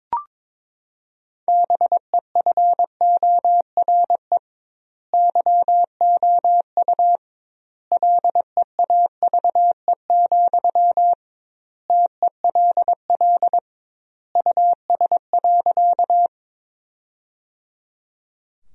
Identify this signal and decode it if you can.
Morse code, heard as a keyed tone, about 22 wpm, 700 Hz. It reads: BEFORE YOU LEAVE, TELL US.
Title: 2x Word-Spacing - Sentences from Top 400 Words - 22wpm